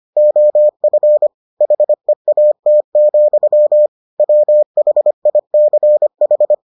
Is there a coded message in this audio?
Yes: Morse code from 0.2 to 6.5 s, heard as a keyed tone, about 25 wpm, 600 Hz. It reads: OF HEAT, WHICH